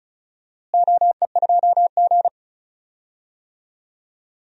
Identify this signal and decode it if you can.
Morse code, heard as a keyed tone, about 35 wpm, 700 Hz. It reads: OE2G
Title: DXCC-100 Call Signs - 4 Characters - Repeated 3x - 35wpm